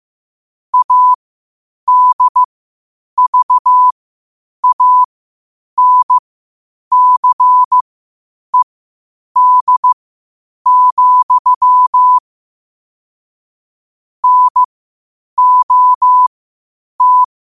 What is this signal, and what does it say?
Morse code, heard as a keyed tone, about 15 wpm, spaced as for 8 wpm, 1000 Hz. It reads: ADVANCED, NOT
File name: qotdfiles / 8wpm